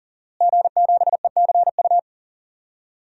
Morse code, heard as a keyed tone, about 40 wpm, 700 Hz. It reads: G7ECU
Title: DXCC-100 Call Signs - 5 Characters - Repeated 3x - 40wpm